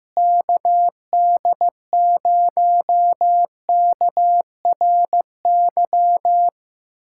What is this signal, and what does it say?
Morse code, heard as a keyed tone, about 15 wpm, 700 Hz. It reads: KD0KRY
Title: USA Callsign DE Callsign - 6 Characters - 15wpm